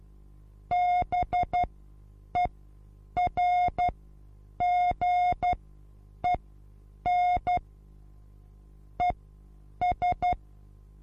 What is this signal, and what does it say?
Morse code, heard as a keyed tone, about 12 wpm, spaced as for 8 wpm, 720 Hz. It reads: BERGEN ES
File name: repos / Morsekurs / DJ4UF Morsen23